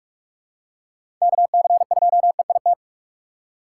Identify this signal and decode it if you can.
Morse code, heard as a keyed tone, about 43 wpm, 700 Hz. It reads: KC2EIT